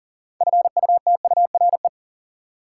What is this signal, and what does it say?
Morse code, heard as a keyed tone, about 37 wpm, 700 Hz. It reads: FUTURE
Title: Sentences from Top 700 Words - New Word Drill - 40wpm